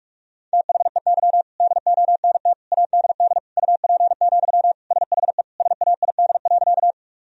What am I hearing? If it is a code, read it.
Morse code, heard as a keyed tone, about 46 wpm, 700 Hz. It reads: THEY DONT ADD UP, SHE SAID.
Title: Sentences from Top 1000 Words - Review Part 1 - 45wpm